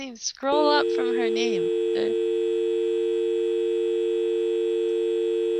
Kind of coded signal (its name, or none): telephone call progress tone